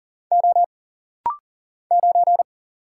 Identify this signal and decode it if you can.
Morse code, heard as a keyed tone, about 40 wpm, 700 Hz. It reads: O 9
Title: Single Letter-Number - Rapid-Fire 40wpm